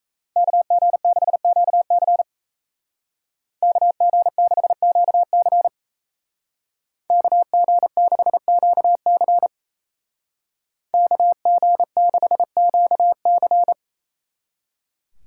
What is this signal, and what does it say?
Morse code, heard as a keyed tone, about 34 wpm, 700 Hz. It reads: KG6QC KG6QC KG6QC KG6QC
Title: Speed Racing - USA 2x2 Call Signs 28wpm